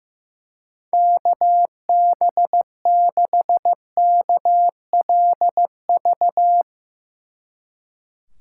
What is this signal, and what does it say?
Morse code, heard as a keyed tone, about 15 wpm, 700 Hz. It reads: KB6KLV
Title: Speed Racing - USA Call Signs 15wpm